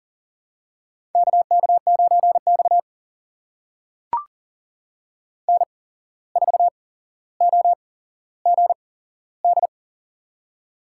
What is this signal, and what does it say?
Morse code, heard as a keyed tone, about 40 wpm, spaced as for 16 wpm, 700 Hz. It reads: KK9X N4OGD
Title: USA Call Signs - 8x Character Spacing - 40wpm